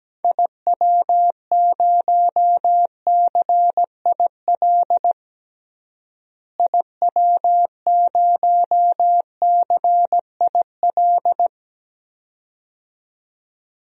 Morse code, heard as a keyed tone, about 17 wpm, 700 Hz. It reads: IW0CIL IW0CIL
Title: DXCC-100 Call Signs - 6 Characters - Repeated 2x - 17wpm